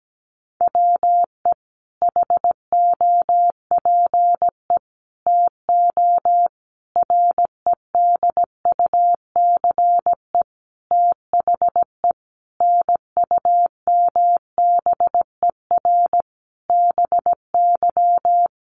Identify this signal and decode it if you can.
Morse code, heard as a keyed tone, about 17 wpm, 700 Hz. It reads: WE HOPE TO REDUCE THE NUMBER BY
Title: Sentences from Top 1100 Words 17wpm